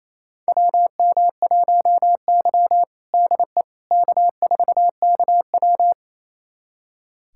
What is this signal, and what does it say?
Morse code, heard as a keyed tone, about 28 wpm, 700 Hz. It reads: WM1Y DE K4KW
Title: USA Callsign DE Callsign - 4 Characters - 28wpm